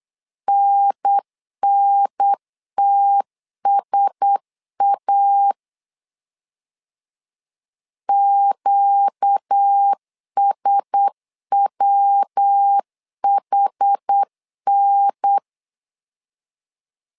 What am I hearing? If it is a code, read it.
Morse code, heard as a keyed tone, about 8 wpm, 800 Hz. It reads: NNTSA QSWHN
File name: LEZIONI / 4 Q1S 8wpm 2710